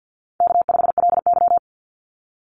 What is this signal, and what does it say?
Morse code, heard as a keyed tone, about 50 wpm, 700 Hz. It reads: K5LY
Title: Easy Call Signs Repeated 3x - 50wpm